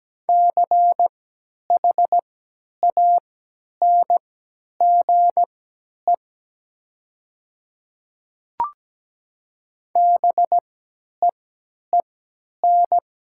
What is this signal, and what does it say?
Morse code, heard as a keyed tone, about 17 wpm, spaced as for 10 wpm, 700 Hz. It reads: CHANGE BEEN
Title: 3x Character Spacing - Top 300 Words - 17wpm